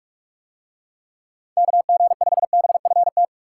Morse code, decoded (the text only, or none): KG5BFT